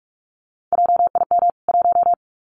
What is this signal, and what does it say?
Morse code, heard as a keyed tone, about 46 wpm, 700 Hz. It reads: JIM 1
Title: CWT Exchange 45wpm